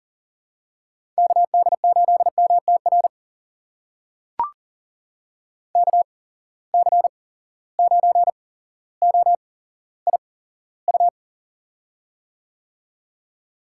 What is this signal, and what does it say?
Morse code, heard as a keyed tone, about 42 wpm, spaced as for 17 wpm, 700 Hz. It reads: KD8MTR KC9OIU
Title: USA Call Signs - 6 Characters - 8x Character Spacing - 40wpm